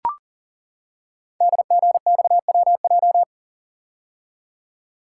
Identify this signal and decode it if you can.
Morse code, heard as a keyed tone, about 40 wpm, 700 Hz. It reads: DGXWJ
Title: Code Groups - 5 Letters - 40wpm